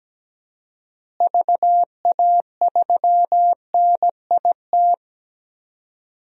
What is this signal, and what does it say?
Morse code, heard as a keyed tone, about 17 wpm, 700 Hz. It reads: VA3NIT
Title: DXCC-100 Call Signs - 6 Characters - Repeated 2x - 17wpm